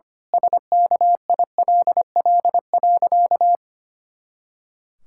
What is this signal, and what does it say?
Morse code, heard as a keyed tone, about 25 wpm, 700 Hz. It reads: SKILL.
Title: Pangram Challenge v2 - 25wpm